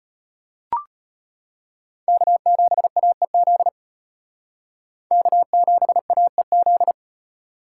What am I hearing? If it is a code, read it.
Morse code, heard as a keyed tone, about 32 wpm, 700 Hz. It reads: K7AEZ K7AEZ